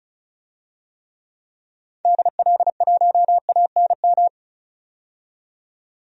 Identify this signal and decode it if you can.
Morse code, heard as a keyed tone, about 36 wpm, 700 Hz. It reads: DL1ANM
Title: DXCC-100 Call Signs - 6 Characters - 35wpm